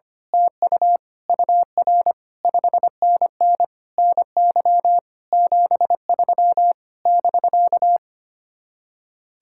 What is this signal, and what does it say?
Morse code, heard as a keyed tone, about 24 wpm, 700 Hz. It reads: TU UR 5NN NY 73 <BK>
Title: Speed-Racing Common QSO Phrases - 25wpm